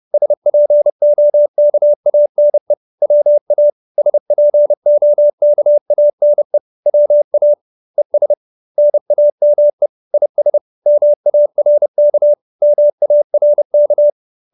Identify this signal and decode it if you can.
Morse code, heard as a keyed tone, about 30 wpm, 590 Hz. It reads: SPOKANE WA SPOKANE WA ES NAME IS MARK MARK